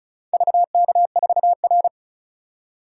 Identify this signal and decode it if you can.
Morse code, heard as a keyed tone, about 36 wpm, 700 Hz. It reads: VK4R
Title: DXCC-100 Call Signs - 4 Characters - 35wpm